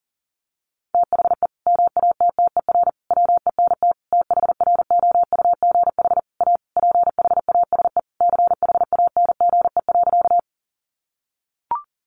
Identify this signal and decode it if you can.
Morse code, heard as a keyed tone, about 37 wpm, 700 Hz. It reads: THE MATTER WENT THROUGH A PHASE CHANGE.
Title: Sentences from Top 1200 Words 40wpm